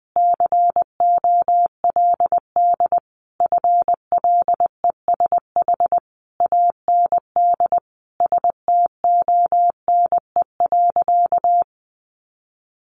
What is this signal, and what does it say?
Morse code, heard as a keyed tone, about 20 wpm, 700 Hz. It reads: COLD FLESH AND STONE.
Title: A Sound Of Thunder 20wpm